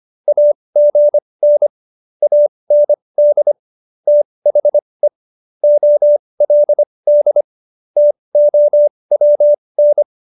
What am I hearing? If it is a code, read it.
Morse code, heard as a keyed tone, about 25 wpm, spaced as for 20 wpm, 590 Hz. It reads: AGN AND THE OLD TOWN